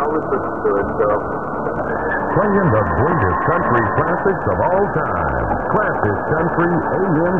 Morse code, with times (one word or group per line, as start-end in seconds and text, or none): none